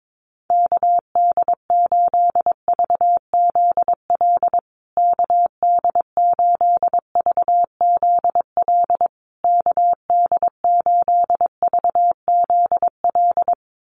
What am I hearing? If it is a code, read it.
Morse code, heard as a keyed tone, about 22 wpm, 700 Hz. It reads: KD8VZL KD8VZL KD8VZL
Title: USA Call Signs Repeated 3x 22wpm